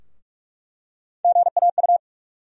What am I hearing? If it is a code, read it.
Morse code, heard as a keyed tone, about 47 wpm, 700 Hz. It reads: GAU
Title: USA Call Sign Postfixes - 3 Letters 45wpm